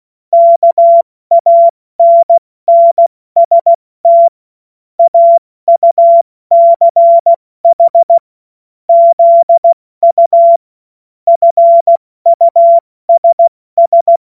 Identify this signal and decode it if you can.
Morse code, heard as a keyed tone, about 16 wpm, 680 Hz. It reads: KANNST AUCH ZU FUSS
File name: Witz-08-Juli-2025-70-F80